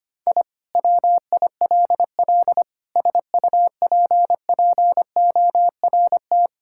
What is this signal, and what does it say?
Morse code, heard as a keyed tone, about 25 wpm, 700 Hz. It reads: I WILL SUPPORT
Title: Sentences from Top 1000 Words - Review Part 5 - 25wpm